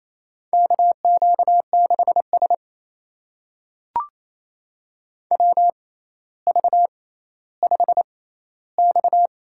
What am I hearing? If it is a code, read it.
Morse code, heard as a keyed tone, about 28 wpm, spaced as for 14 wpm, 700 Hz. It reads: KQ6S WV5X
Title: USA Call Signs - 4 Characters - 6x Character Spacing - 28wpm